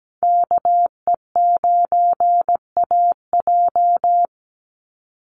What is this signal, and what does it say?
Morse code, heard as a keyed tone, about 17 wpm, 700 Hz. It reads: KE9AJ
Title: NA SOTA Chaser Callsigns 17wpm